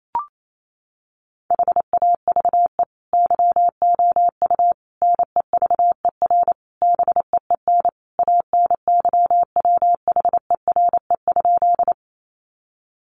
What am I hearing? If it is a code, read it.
Morse code, heard as a keyed tone, about 28 wpm, 700 Hz. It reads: HAVE YOU NEVER BEEN ANYWHERE?
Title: Sentences from Top 1400 Words 28wpm